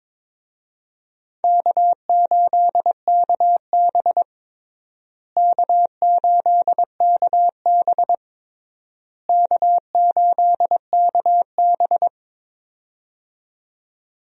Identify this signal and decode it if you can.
Morse code, heard as a keyed tone, about 22 wpm, 700 Hz. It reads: K8KB K8KB K8KB
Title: USA Call Signs - 4 Characters - Repeated 3x - 22wpm